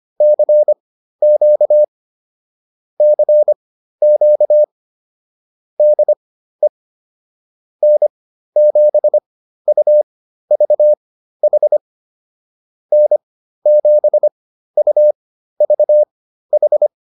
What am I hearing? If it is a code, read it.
Morse code, heard as a keyed tone, about 25 wpm, spaced as for 13 wpm, 600 Hz. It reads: CQ CQ DE N7UVH N7UVH